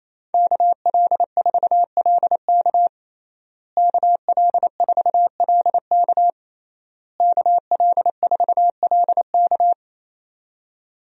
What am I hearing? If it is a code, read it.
Morse code, heard as a keyed tone, about 28 wpm, 700 Hz. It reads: KL4LK KL4LK KL4LK
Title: USA Call Signs - 5 Characters - Repeated 3x - 28wpm